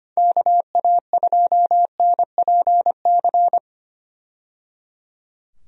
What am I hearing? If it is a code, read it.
Morse code, heard as a keyed tone, about 25 wpm, 700 Hz. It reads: KA2NPC
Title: USA Call Signs - 6 Characters - 25wpm